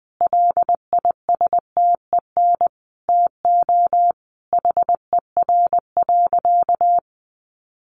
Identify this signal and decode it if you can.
Morse code, heard as a keyed tone, about 20 wpm, 700 Hz. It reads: LISTEN TO HER.
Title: Sentences from Top 1300 Words 20wpm